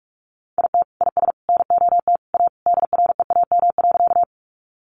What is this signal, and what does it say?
Morse code, heard as a keyed tone, about 44 wpm, 700 Hz. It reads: IT IS NOT A DREAM.
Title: Sentences from Top 1500 Words 45wpm